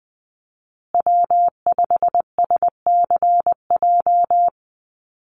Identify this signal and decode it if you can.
Morse code, heard as a keyed tone, about 20 wpm, 700 Hz. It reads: W5SCJ
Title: USA Call Signs Repeated 2x 20wpm